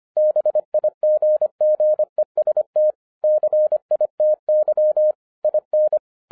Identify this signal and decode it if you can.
Morse code, heard as a keyed tone, about 25 wpm, 610 Hz. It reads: BIGGEST CITY IN
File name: 2011-10-mp3 / n04-JUST MOVED 25wpm